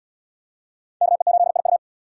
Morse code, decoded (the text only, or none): KE8IU